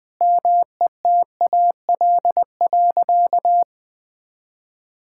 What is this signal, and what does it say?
Morse code, heard as a keyed tone, about 20 wpm, 700 Hz. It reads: METAL.
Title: Sentences from Top 1000 Words - Review Part 6 - 20wpm